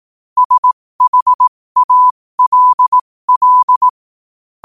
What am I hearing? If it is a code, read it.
Morse code, heard as a keyed tone, about 18 wpm, spaced as for 15 wpm, 1000 Hz. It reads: SHALL
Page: New quotes every day in morse code at 15 Words per minute.